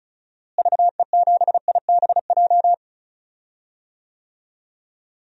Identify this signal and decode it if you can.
Morse code, heard as a keyed tone, about 34 wpm, 700 Hz. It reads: VE7IBJ
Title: DXCC-100 Call Signs - 6 Characters - Repeated 3x - 35wpm